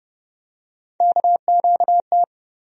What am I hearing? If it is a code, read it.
Morse code, heard as a keyed tone, about 30 wpm, 700 Hz. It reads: KQT
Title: USA Call Sign Postfixes - 3 Letters 30wpm